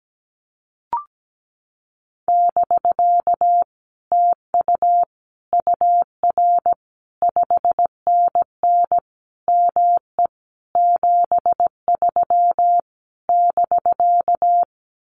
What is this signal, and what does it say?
Morse code, heard as a keyed tone, about 17 wpm, 700 Hz. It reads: <BK> TU UR 5NN ME 73 <BK>
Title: Common QSO Phrases Encore 17wpm